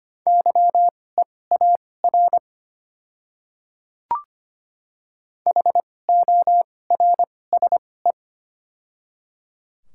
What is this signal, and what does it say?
Morse code, heard as a keyed tone, about 25 wpm, spaced as for 18 wpm, 700 Hz. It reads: YEAR HORSE
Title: 2x Character Spacing - Top 300 Words - 25wpm